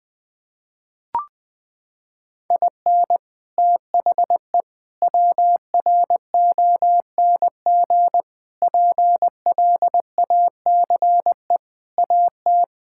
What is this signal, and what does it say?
Morse code, heard as a keyed tone, about 21 wpm, 700 Hz. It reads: IN THE WRONG PLACE AT
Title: Sets of 8 Words - 20wpm